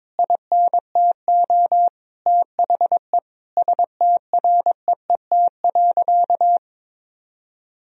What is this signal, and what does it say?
Morse code, heard as a keyed tone, about 22 wpm, 700 Hz. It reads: INTO THE STREET.
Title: Sentences from Top 1000 Words - Review Part 4 - 22wpm